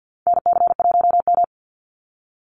Morse code, heard as a keyed tone, about 50 wpm, 700 Hz. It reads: NC1M
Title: Call Signs Repeated 2x - 50wpm